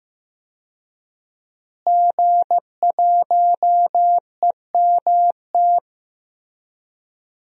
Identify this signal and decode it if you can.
Morse code, heard as a keyed tone, about 16 wpm, 700 Hz. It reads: G1EMT
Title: DXCC-100 Call Signs - 5 Characters - Repeated 2x - 15wpm